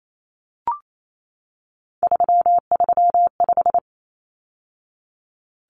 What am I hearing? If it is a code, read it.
Morse code, heard as a keyed tone, about 28 wpm, 700 Hz. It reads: 335
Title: Numbers - 3-Digit 28wpm